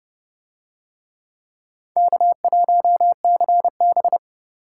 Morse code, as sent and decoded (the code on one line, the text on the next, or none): -.- .---- -.-. -...
K1CB